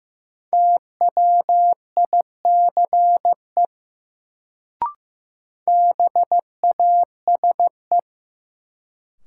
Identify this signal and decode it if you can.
Morse code, heard as a keyed tone, about 15 wpm, 700 Hz. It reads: TWICE BASE
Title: Sentences from Top 1600 Words - New Word Drill Alt - 15wpm